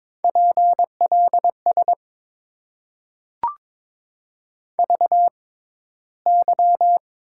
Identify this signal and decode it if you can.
Morse code, heard as a keyed tone, about 22 wpm, spaced as for 13 wpm, 700 Hz. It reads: PLS VY
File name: Top 90 QSO Elements - 6x Character Spacing - 22wpm